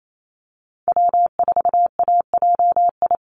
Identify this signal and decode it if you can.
Morse code, heard as a keyed tone, about 28 wpm, 700 Hz. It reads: W4AJI
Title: USA Call Signs 28wpm